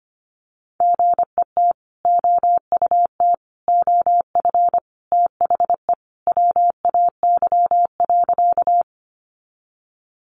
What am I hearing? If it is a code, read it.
Morse code, heard as a keyed tone, about 25 wpm, 700 Hz. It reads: GET OUT OF THE WAY.
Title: Sentences from Top 200 Words 25wpm